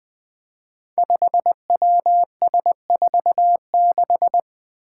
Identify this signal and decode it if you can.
Morse code, heard as a keyed tone, about 20 wpm, 700 Hz. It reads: 5WS46
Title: Five Challenging Letter-Number Combinations 20wpm